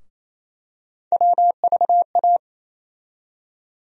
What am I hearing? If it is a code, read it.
Morse code, heard as a keyed tone, about 27 wpm, 700 Hz. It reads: WVA
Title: USA Call Sign Postfixes - 3 Letters 28wpm